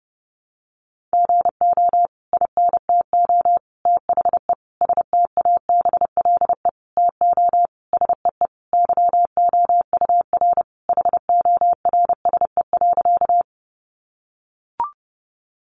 Morse code, heard as a keyed tone, about 30 wpm, 700 Hz. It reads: GO INTO THE STABLE TO SEE YOUR HORSE.
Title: Sentences from Top 1500 Words 30wpm